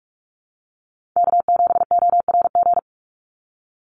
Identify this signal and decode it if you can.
Morse code, heard as a keyed tone, about 44 wpm, 690 Hz. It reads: K7ORG
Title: NA SOTA Chaser Callsigns 45wpm